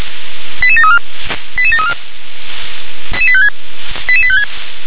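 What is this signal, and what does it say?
Five-tone selective call (ZVEI), heard as quick runs of five tones, tones of about 70 ms. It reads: 88734 88734 88856
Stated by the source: Five-tone ZVEI selective call 88748, received by SDR.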